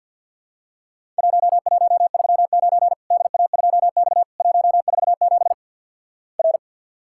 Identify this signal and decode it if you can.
Morse code, heard as a keyed tone, about 50 wpm, 700 Hz. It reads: 1139 DA2X 147 R
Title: Worked All Europe DX Contest - QTC Exchange with Repeats - 50wpm